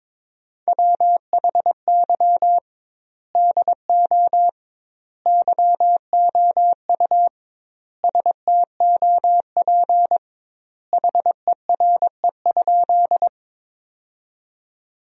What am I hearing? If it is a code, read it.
Morse code, heard as a keyed tone, about 22 wpm, 700 Hz. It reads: WHY DO YOU STOP HERE?
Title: 2x Word-Spacing - Sentences from Top 300 Words - 22wpm